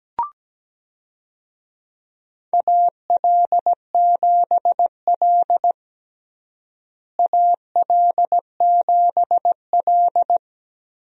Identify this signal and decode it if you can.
Morse code, heard as a keyed tone, about 17 wpm, 700 Hz. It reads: AL7L AL7L
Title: USA Call Signs - 4 Characters - Repeated 3x - 17wpm